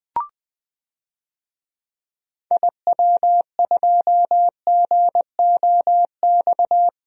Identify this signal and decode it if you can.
Morse code, heard as a keyed tone, about 20 wpm, 700 Hz. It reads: IW2GOX
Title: DXCC-100 Call Signs - 6 Characters - Repeated 2x - 20wpm